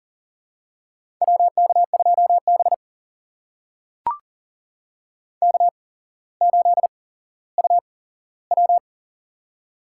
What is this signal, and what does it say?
Morse code, heard as a keyed tone, about 40 wpm, spaced as for 17 wpm, 700 Hz. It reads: WK2B K8UW
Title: USA Call Signs - 4 Characters - 8x Character Spacing - 40wpm